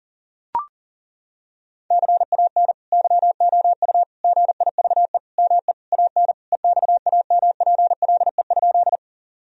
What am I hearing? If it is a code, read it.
Morse code, heard as a keyed tone, about 40 wpm, 700 Hz. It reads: CAN YOU GIVE ME AN EXAMPLE?
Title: Sentences from Top 1000 Words - Review Part 1 - 40wpm